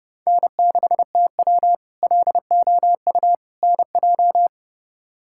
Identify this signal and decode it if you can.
Morse code, heard as a keyed tone, about 32 wpm, 700 Hz. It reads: N6TW LOU NJ
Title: K1USN SST QSO Practice - 30wpm